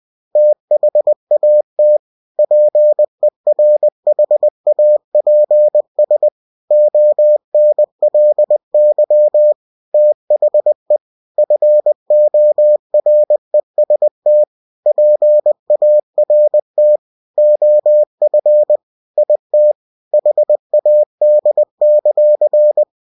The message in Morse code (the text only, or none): THAT PERHAPS ONLY THE FOREST PART OF IT HAD;